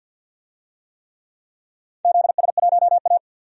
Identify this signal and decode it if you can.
Morse code, heard as a keyed tone, about 50 wpm, 700 Hz. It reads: ZS1A
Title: DXCC-100 Call Signs - 4 Characters - 50wpm